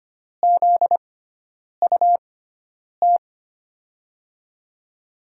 Morse code, heard as a keyed tone, about 25 wpm, spaced as for 9 wpm, 700 Hz. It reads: ZUT
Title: USA Call Signs - 5 Characters - 6x Character Spacing - 25wpm